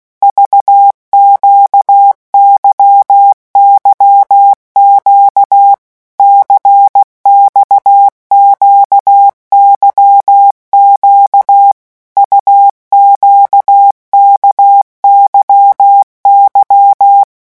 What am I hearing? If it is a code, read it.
Morse code, heard as a keyed tone, about 16 wpm, 800 Hz. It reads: VQYYQ CXQYQ UQKYY
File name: morse mp3 / revision2-vitesse 15 mots